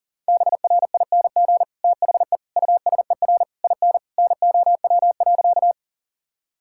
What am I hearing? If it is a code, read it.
Morse code, heard as a keyed tone, about 40 wpm, 700 Hz. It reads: BRING THE USER IN NOW.
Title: Sentences from Top 1000 Words - Review Part 5 - 40wpm